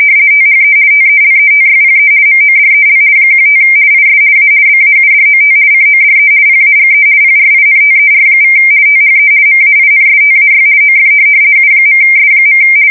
RTTY, about 50 baud. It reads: THE TELETYPE MODEL 15 IS A BAUDOT CODE PAGE PRINTER= THE MAINSTAY OF U.S.
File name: fsk170-50B / teletype-corp